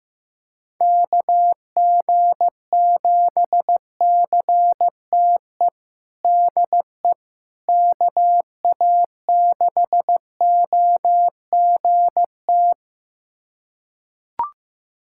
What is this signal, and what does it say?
Morse code, heard as a keyed tone, about 15 wpm, 700 Hz. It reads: KG7CTE DE KA6OGT
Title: USA Callsign DE Callsign - 6 Characters - 15wpm